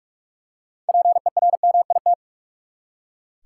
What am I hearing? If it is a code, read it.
Morse code, heard as a keyed tone, about 45 wpm, 700 Hz. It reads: PERMIT
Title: Six Letter Words - 45wpm